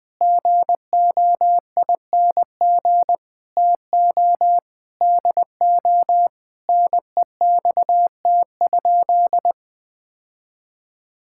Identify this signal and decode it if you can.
Morse code, heard as a keyed tone, about 20 wpm, 700 Hz. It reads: GOING TO DO NEXT?
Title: Sentences with 5-Word Phrases - 20wpm